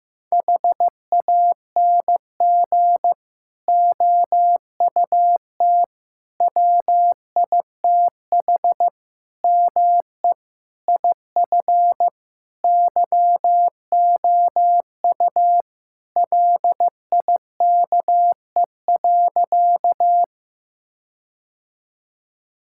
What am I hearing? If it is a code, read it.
Morse code, heard as a keyed tone, about 15 wpm, 700 Hz. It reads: HANG OUT WITH ME IF YOU LIKE.
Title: Sentences from Top 1600 Words 15wpm